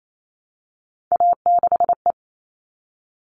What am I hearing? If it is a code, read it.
Morse code, heard as a keyed tone, about 28 wpm, 700 Hz. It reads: A6E
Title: All Cut Numbers - 3-Digit 28wpm